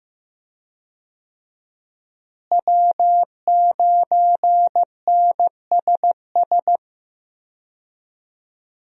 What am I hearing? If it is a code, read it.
Morse code, heard as a keyed tone, about 15 wpm, 700 Hz. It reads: W9NSS
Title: USA Call Signs - 5 Characters - Repeated 3x - 15wpm